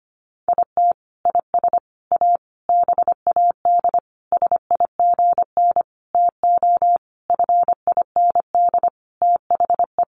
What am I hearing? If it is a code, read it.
Morse code, heard as a keyed tone, about 25 wpm, 700 Hz. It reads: IT IS A BAD SIGN TO FIND THE
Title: Sentences from Top 1000 Words 25wpm